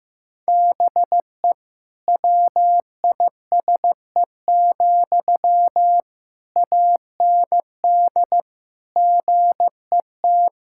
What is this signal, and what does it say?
Morse code, heard as a keyed tone, about 15 wpm, 700 Hz. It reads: BE WISE, AND GET
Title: Sentences from Top 1600 Words 15wpm